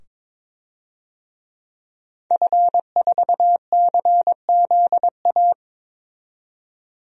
Morse code, heard as a keyed tone, about 22 wpm, 700 Hz. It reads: F4CZA
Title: DXCC-100 Call Signs - 5 Characters - 22wpm